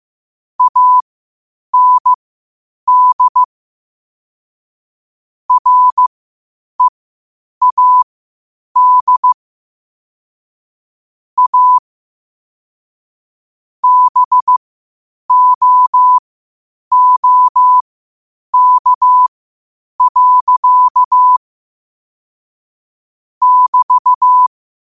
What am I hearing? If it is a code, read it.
Morse code, heard as a keyed tone, about 15 wpm, spaced as for 8 wpm, 1000 Hz. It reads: AND READ A BOOK. <BT>
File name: qotdfiles / 8wpm